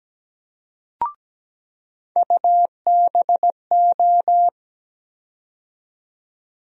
Morse code, heard as a keyed tone, about 17 wpm, 700 Hz. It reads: UBO
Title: USA Call Sign Postfixes - 3 Letters 17wpm